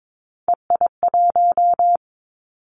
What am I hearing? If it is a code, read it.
Morse code, heard as a keyed tone, about 22 wpm, 700 Hz. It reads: EI1
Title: Callsign Prefixes for Top 50 DXCC - 22wpm